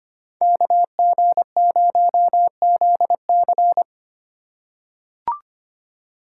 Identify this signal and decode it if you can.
Morse code, heard as a keyed tone, about 25 wpm, 690 Hz. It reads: KG0ZC
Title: USA Call Signs - 5 Characters - 6x Character Spacing - 25wpm